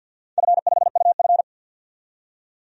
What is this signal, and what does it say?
Morse code, heard as a keyed tone, about 50 wpm, 700 Hz. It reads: F5UF
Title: DXCC-100 Call Signs - 4 Characters - 50wpm